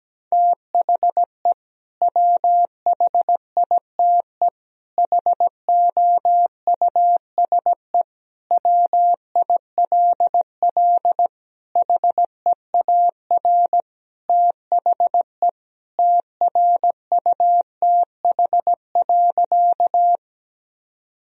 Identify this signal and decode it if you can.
Morse code, heard as a keyed tone, about 17 wpm, 700 Hz. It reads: THE WHITE HOUSE WILL HEAR THE TRUTH.
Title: Sentences from Top 1000 Words - Review Part 1 - 17wpm